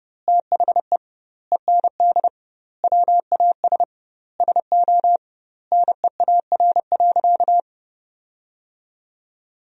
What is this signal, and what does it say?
Morse code, heard as a keyed tone, about 30 wpm, 700 Hz. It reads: THE END WAS SO NEAR.
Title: 2x Word-Spacing - Sentences from Top 300 Words - 30wpm